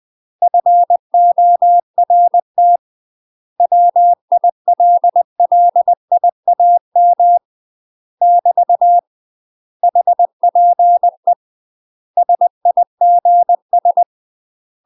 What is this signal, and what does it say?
Morse code, heard as a keyed tone, about 20 wpm, 700 Hz. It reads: FORT WILLIAM <BT> HPE SIGS
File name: UK MASTER QSOs TRAINING / 20WPM / UK MASTER QSOs 0026